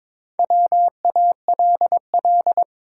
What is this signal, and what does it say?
Morse code, heard as a keyed tone, about 22 wpm, 700 Hz. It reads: WALL
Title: Sentences from Top 1100 Words - New Word Drill Alt - 22wpm